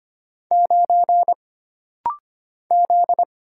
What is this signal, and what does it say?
Morse code, heard as a keyed tone, about 25 wpm, 700 Hz. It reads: 9 Z
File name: Single Character (Letter:Number:Punctuation:Prosign) - Rapid-Fire - 25wpm